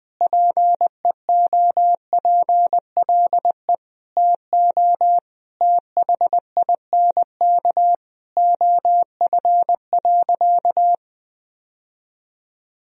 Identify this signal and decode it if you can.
Morse code, heard as a keyed tone, about 20 wpm, 700 Hz. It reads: PEOPLE TO THINK OF.
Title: Sentences from Top 1000 Words - Review Part 1 - 20wpm